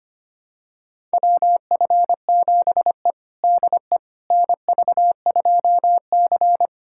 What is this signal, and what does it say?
Morse code, heard as a keyed tone, about 25 wpm, 700 Hz. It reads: WF7E DE NV2C
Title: USA Callsign DE Callsign - 4 Characters - 25wpm